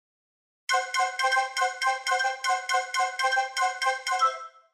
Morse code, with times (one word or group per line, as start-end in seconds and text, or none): none